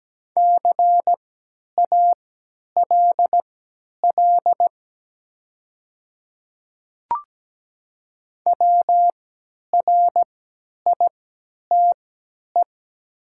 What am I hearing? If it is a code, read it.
Morse code, heard as a keyed tone, about 17 wpm, spaced as for 10 wpm, 700 Hz. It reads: CALL WRITE
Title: 3x Character Spacing - Top 100 Words - 17wpm